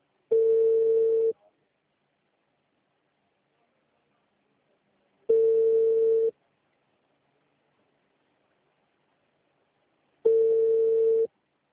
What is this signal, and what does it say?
Ringback tone (China, 450 Hz).